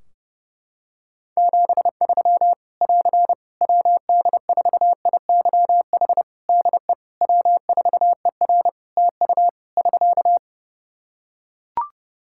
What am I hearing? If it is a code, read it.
Morse code, heard as a keyed tone, about 30 wpm, 700 Hz. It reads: 73 <AR> WD4IYH DE W4ER TU <SK>
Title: Practice QSO - Exchange 4 - Ending - 30wpm